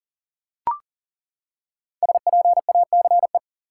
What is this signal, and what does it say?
Morse code, heard as a keyed tone, about 37 wpm, 700 Hz. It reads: SPACE
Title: Five Letter Words - 40wpm